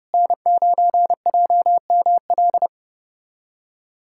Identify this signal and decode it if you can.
Morse code, heard as a keyed tone, about 30 wpm, 700 Hz. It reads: N9JML
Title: USA Callsign DE Callsign - 5 Characters - 30wpm